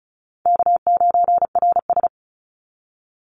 Morse code, heard as a keyed tone, about 34 wpm, 700 Hz. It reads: K9RS
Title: Easy Call Signs Repeated 2x - 35wpm